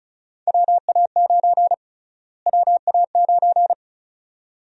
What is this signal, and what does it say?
Morse code, heard as a keyed tone, about 36 wpm, 700 Hz. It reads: WA9 WA9
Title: USA Call Sign Prefixes - Repeated 2x - 35wpm